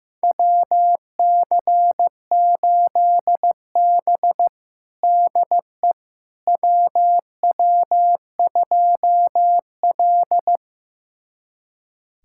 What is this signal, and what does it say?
Morse code, heard as a keyed tone, about 15 wpm, 700 Hz. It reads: WC8B DE WW2L
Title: USA Callsign DE Callsign - 4 Characters - 15wpm